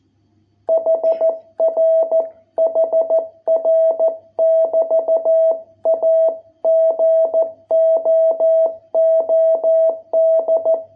Morse code, heard as a keyed tone, about 14 wpm, 660 Hz. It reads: HR HR <BT> A GOOD